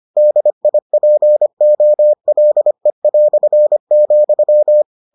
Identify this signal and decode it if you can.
Morse code, heard as a keyed tone, about 25 wpm, 600 Hz. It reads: DIPOLE",